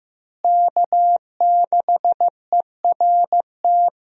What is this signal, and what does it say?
Morse code, heard as a keyed tone, about 15 wpm, 700 Hz. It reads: K6ERT
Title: USA Call Signs - 5 Characters - Repeated 3x - 15wpm